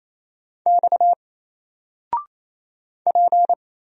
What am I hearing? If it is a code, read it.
Morse code, heard as a keyed tone, about 27 wpm, 700 Hz. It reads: X P
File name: Single-Letter-Number-28wpm